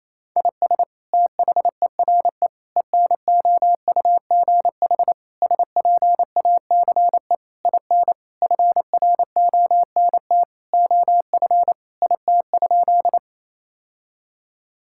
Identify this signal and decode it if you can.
Morse code, heard as a keyed tone, about 27 wpm, 700 Hz. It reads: IS THERE ENOUGH SPACE IN FRONT OF IT?
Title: Sentences from Top 1000 Words - Review Part 5 - 28wpm